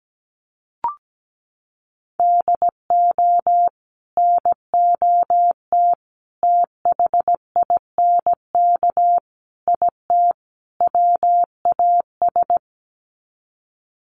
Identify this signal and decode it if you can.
Morse code, heard as a keyed tone, about 17 wpm, 700 Hz. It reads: DO NOT THINK IT WAS
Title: Sets of 5 Words Encore 17wpm